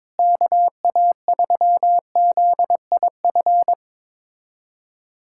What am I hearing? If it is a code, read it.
Morse code, heard as a keyed tone, about 22 wpm, 700 Hz. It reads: KA3ZIF
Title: USA Callsign DE Callsign 22wpm